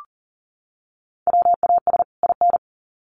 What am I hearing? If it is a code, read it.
Morse code, heard as a keyed tone, about 40 wpm, 700 Hz. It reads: WAS IN
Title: Sets of 2 Words 40wpm